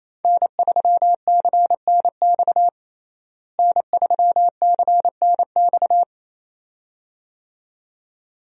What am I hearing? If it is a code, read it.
Morse code, heard as a keyed tone, about 28 wpm, 700 Hz. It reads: N3CNX N3CNX
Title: USA Call Signs - 5 Characters - Repeated 2x - 28wpm